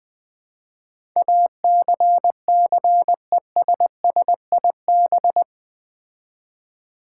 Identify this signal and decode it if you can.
Morse code, heard as a keyed tone, about 20 wpm, 700 Hz. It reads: ACCESSIB
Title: Build A Word - 4 Letters At A Time - 10-Letter Words - 20wpm